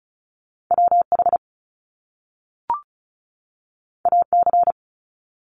Morse code, heard as a keyed tone, about 36 wpm, 700 Hz. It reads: WH AC
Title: Top Two Letter Combinations in Words 35wpm